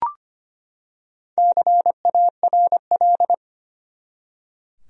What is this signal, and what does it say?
Morse code, heard as a keyed tone, about 25 wpm, 700 Hz. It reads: CARL
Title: USA Common Names - 25wpm